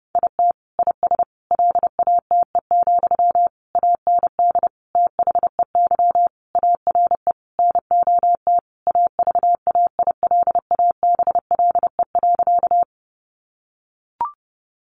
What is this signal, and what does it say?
Morse code, heard as a keyed tone, about 30 wpm, 700 Hz. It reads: IT IS LATE, AND THEY ARE NOT AVAILABLE.
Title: Sentences from Top 600 Words 30wpm